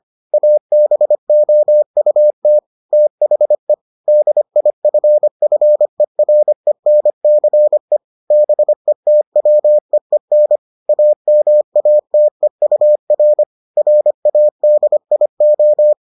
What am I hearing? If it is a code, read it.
Morse code, heard as a keyed tone, about 25 wpm, 600 Hz. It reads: ABOUT THE DIFFERENCE BETWEEN AMATEUR RADIO